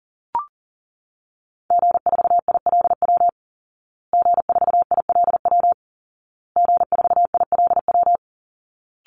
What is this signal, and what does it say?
Morse code, heard as a keyed tone, about 40 wpm, 700 Hz. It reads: G4ILW G4ILW G4ILW
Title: Call Signs Repeated 2x - 40wpm